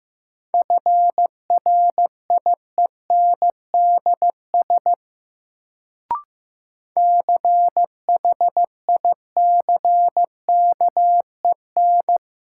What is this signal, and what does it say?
Morse code, heard as a keyed tone, about 15 wpm, 700 Hz. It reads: FRIENDS CHICKEN
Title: Sentences from Top 1000 Words - Review Part 5 - New Word Drill - 15wpm